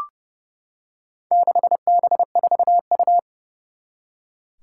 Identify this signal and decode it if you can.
Morse code, heard as a keyed tone, about 30 wpm, 700 Hz. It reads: 6B4U
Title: Four Challenging Letter-Number Combinations 30wpm